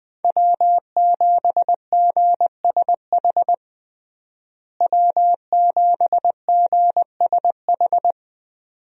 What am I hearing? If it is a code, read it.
Morse code, heard as a keyed tone, about 20 wpm, 700 Hz. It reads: W7GSH W7GSH
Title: USA Call Signs - 5 Characters - Repeated 2x - 20wpm